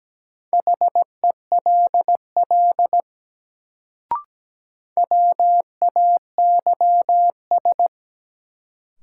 Morse code, heard as a keyed tone, about 17 wpm, 700 Hz. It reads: HELL WAYS
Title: Sets of 5 Words Encore - New Word Drill - 17wpm